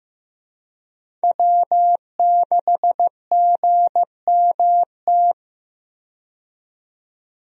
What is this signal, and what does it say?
Morse code, heard as a keyed tone, about 15 wpm, 700 Hz. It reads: W6GMT
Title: USA Call Signs - 5 Characters - 15wpm